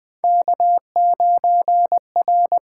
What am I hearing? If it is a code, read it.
Morse code, heard as a keyed tone, about 20 wpm, 700 Hz. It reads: K9R
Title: USA Call Signs - 5 Characters - 8x Character Spacing - 20wpm